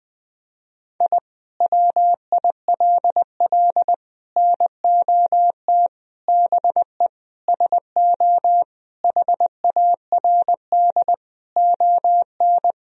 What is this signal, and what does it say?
Morse code, heard as a keyed tone, about 20 wpm, 700 Hz. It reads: I WILL NOT BE SO HARD ON
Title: Sentences from Top 1000 Words - Review Part 1 - 20wpm